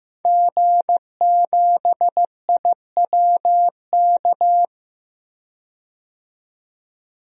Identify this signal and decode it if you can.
Morse code, heard as a keyed tone, about 15 wpm, 700 Hz. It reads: G7IWK
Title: DXCC-100 Call Signs - 5 Characters - 15wpm